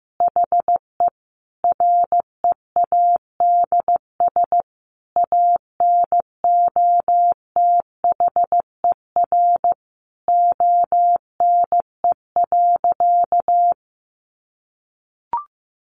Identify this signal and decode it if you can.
Morse code, heard as a keyed tone, about 15 wpm, 700 Hz. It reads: HE READS ANOTHER ONE.
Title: Additional Easy Phrases 15wpm